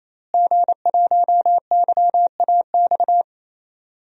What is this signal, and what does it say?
Morse code, heard as a keyed tone, about 28 wpm, 700 Hz. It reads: G1YAX
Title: DXCC-100 Call Signs - 5 Characters - Repeated 3x - 28wpm